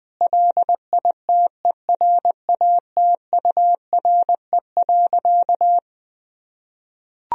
Morse code, heard as a keyed tone, about 21 wpm, 700 Hz. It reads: LITERATURE.
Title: Sentences from Top 1000 Words - Review Part 3 - 20wpm